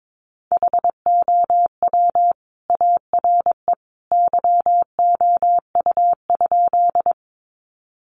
Morse code, heard as a keyed tone, about 22 wpm, 700 Hz. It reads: HOW ARE YOU?
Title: Sentences from Top 100 Words 22wpm